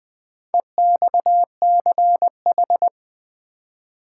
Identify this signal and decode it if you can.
Morse code, heard as a keyed tone, about 20 wpm, 700 Hz. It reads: EXCH
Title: Build A Word - 2 Letters At A Time - 8-Letter Words - 20wpm